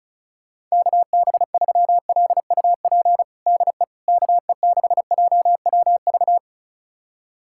Morse code, heard as a keyed tone, about 36 wpm, 690 Hz. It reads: KB3LUP DE KE6JWV
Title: USA Callsign DE Callsign 35wpm